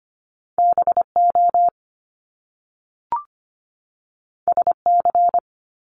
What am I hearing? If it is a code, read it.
Morse code, heard as a keyed tone, about 25 wpm, 700 Hz. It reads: BO SC
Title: Top Two Letter Combinations in Words 25wpm